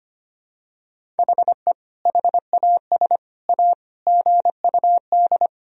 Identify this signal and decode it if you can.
Morse code, heard as a keyed tone, about 25 wpm, 700 Hz. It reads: HE HAS A GUD
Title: Sentences from Top 500 Words and Abbreviations 25wpm